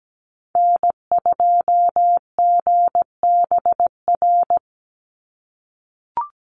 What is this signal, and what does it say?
Morse code, heard as a keyed tone, about 17 wpm, 700 Hz. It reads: N2GBR
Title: NA SOTA Activator Callsigns 17wpm